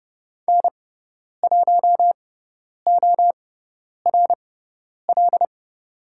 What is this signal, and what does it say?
Morse code, heard as a keyed tone, about 30 wpm, spaced as for 10 wpm, 700 Hz. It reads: N1ORL
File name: Lesson 16g - Letter c - Call Signs 30wpm